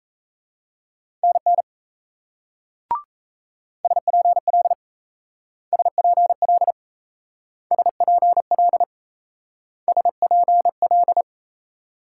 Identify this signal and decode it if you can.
Morse code, heard as a keyed tone, about 35 wpm, 700 Hz. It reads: NN SPL SPL SPL SPL
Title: Speed-Racing Multi-letter Phonemes - 28wpm